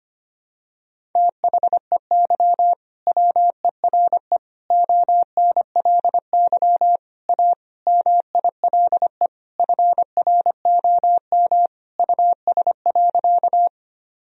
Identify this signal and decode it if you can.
Morse code, heard as a keyed tone, about 25 wpm, 700 Hz. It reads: THEY WERE ONLY A MILE FROM US.
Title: Sentences from Top 1000 Words - Review Part 1 - 25wpm